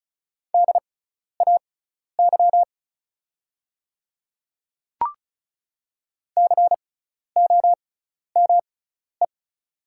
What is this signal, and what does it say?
Morse code, heard as a keyed tone, about 36 wpm, spaced as for 12 wpm, 700 Hz. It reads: DAY COME
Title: 6x Character Spacing - Top 100 Words - 35wpm